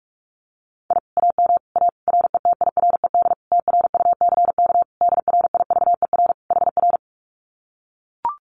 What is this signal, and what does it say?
Morse code, heard as a keyed tone, about 44 wpm, 700 Hz. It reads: I AM A RETIRED TRUCK DRIVER HR
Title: Common QSO Phrases Encore 45wpm